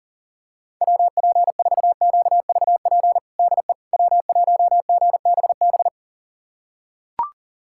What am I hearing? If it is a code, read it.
Morse code, heard as a keyed tone, about 37 wpm, 700 Hz. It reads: WP4QVP DE W1GBB
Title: USA Callsign DE Callsign 40wpm